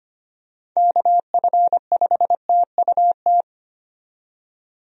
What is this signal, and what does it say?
Morse code, heard as a keyed tone, about 25 wpm, 700 Hz. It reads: KF5TUT
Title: USA Call Signs - 6 Characters - 25wpm